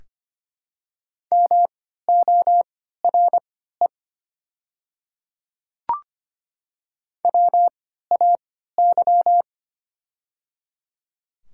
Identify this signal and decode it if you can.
Morse code, heard as a keyed tone, about 25 wpm, spaced as for 14 wpm, 700 Hz. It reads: MORE WAY
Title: 3x Character Spacing - Top 200 Words - 25wpm